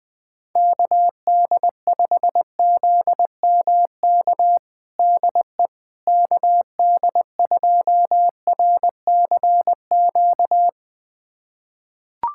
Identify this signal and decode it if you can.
Morse code, heard as a keyed tone, about 20 wpm, 700 Hz. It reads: KD5ZMK DE KD2RCQ
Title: USA Callsign DE Callsign - 6 Characters - 20wpm